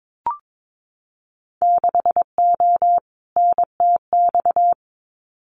Morse code, heard as a keyed tone, about 22 wpm, 700 Hz. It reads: 6O NTX
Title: Winter Field Day Exchanges Encore 22wpm